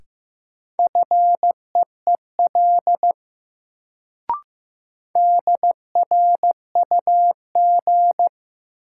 Morse code, heard as a keyed tone, about 15 wpm, 700 Hz. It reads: FEEL DRUG
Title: Sets of 5 Words - New Word Drill - 15wpm